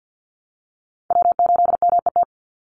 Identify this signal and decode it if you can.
Morse code, heard as a keyed tone, about 48 wpm, 700 Hz. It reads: W8MET
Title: Call Signs Repeated 2x - 50wpm